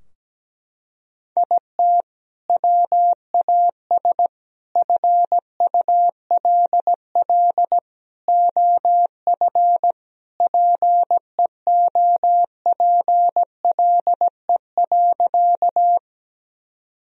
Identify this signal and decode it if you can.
Morse code, heard as a keyed tone, about 17 wpm, 700 Hz. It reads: IT WAS FULL OF PEOPLE.
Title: Sentences from Top 1000 Words - Review Part 3 - 17wpm